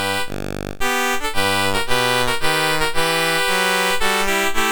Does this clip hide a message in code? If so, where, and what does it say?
No.